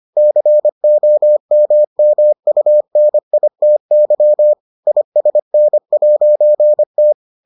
Morse code, heard as a keyed tone, about 25 wpm, 600 Hz. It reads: COMMUNITY ISN'T